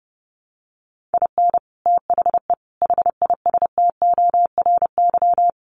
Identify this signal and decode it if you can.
Morse code, heard as a keyed tone, about 30 wpm, 700 Hz. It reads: IN THE HISTORY
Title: Sets of 5 Words 30wpm